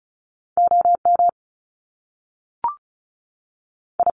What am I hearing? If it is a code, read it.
Morse code, heard as a keyed tone, about 36 wpm, spaced as for 6 wpm, 700 Hz. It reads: OMI
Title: Top Two Letter Combinations in Words 35wpm